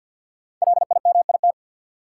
Morse code, heard as a keyed tone, about 53 wpm, 700 Hz. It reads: LIMIT
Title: Sentences from Top 1500 Words - New Word Drill - 50wpm